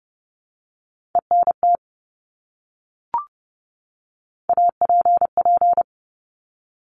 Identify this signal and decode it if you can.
Morse code, heard as a keyed tone, about 30 wpm, 690 Hz. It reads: ENT APP
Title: Top Three Letter Combinations in Words 30wpm